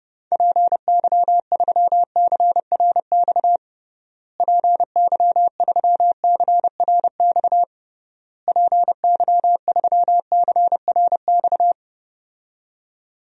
Morse code, heard as a keyed tone, about 32 wpm, 700 Hz. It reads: PY3CRX PY3CRX PY3CRX
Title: DXCC-100 Call Signs - 6 Characters - Repeated 3x - 30wpm